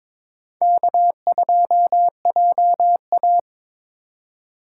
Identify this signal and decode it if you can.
Morse code, heard as a keyed tone, about 26 wpm, 700 Hz. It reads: K2JA
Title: Speed Racing - USA 1x2 Call Signs 22wpm